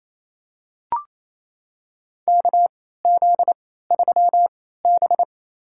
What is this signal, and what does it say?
Morse code, heard as a keyed tone, about 28 wpm, spaced as for 16 wpm, 700 Hz. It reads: KZ3B
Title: USA Call Signs - 4 Characters - 3x Character Spacing - 28wpm